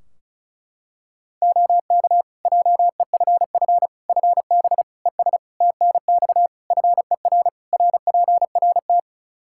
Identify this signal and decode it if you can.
Morse code, heard as a keyed tone, about 34 wpm, 700 Hz. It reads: OK JEFF FB ES TNX FER RPRT
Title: K1USN SST QSO Practice 35wpm